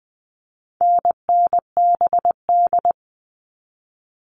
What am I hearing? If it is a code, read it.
Morse code, heard as a keyed tone, about 20 wpm, 700 Hz. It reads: NNBD
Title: All Cut Numbers - 4-Digit 20wpm